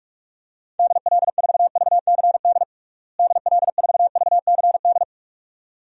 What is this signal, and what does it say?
Morse code, heard as a keyed tone, about 41 wpm, 700 Hz. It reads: DL4VCD DL4VCD
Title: DXCC-100 Call Signs - 6 Characters - Repeated 2x - 45wpm